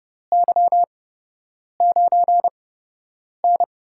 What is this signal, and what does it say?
Morse code, heard as a keyed tone, about 30 wpm, spaced as for 8 wpm, 700 Hz. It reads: Y9N
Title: USA Call Signs - 4 Characters - 8x Character Spacing - 30wpm